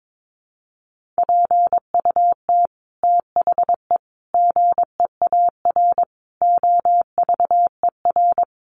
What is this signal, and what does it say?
Morse code, heard as a keyed tone, about 22 wpm, 700 Hz. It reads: PUT THE GEAR OVER
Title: Sentences from Top 1400 Words 22wpm